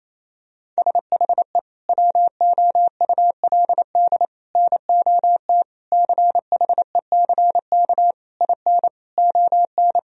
Morse code, heard as a keyed tone, about 28 wpm, 700 Hz. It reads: SHE WOULD NOT CHECK IN ON
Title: Sentences from Top 1000 Words - Review Part 3 - 28wpm